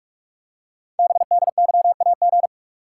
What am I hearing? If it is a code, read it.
Morse code, heard as a keyed tone, about 44 wpm, 690 Hz. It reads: BDYAG
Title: Code Groups - 5 Letters - 45wpm